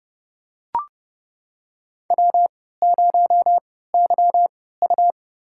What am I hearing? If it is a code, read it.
Morse code, heard as a keyed tone, about 30 wpm, spaced as for 17 wpm, 700 Hz. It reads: W0YU
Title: USA Call Signs - 4 Characters - 3x Character Spacing - 30wpm